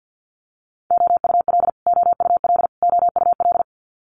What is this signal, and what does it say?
Morse code, heard as a keyed tone, about 50 wpm, 700 Hz. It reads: OUL OUL OUL
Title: Multi-letter Phonemes Repeated 3x 50wpm